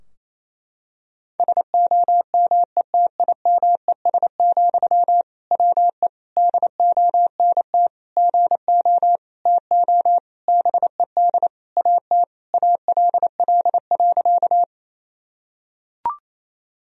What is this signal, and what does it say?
Morse code, heard as a keyed tone, about 27 wpm, 700 Hz. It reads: SOMETIMES, WE DONT GO TO BED AT ALL.
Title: Sentences from Top 1000 Words - Review Part 3 - 28wpm